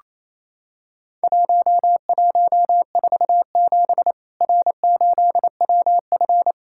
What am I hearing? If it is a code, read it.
Morse code, heard as a keyed tone, about 28 wpm, 700 Hz. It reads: 1147 R8WF
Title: Worked All Europe DX Contest - QTC Exchange - 28wpm